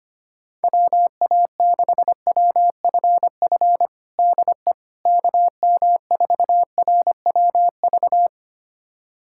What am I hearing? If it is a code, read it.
Morse code, heard as a keyed tone, about 25 wpm, 700 Hz. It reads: WA6WFF DE KM4RWV
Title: USA Callsign DE Callsign 25wpm